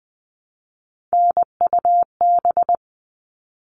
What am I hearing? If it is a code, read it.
Morse code, heard as a keyed tone, about 20 wpm, 700 Hz. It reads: NUB
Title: All Cut Numbers - 3-Digit 20wpm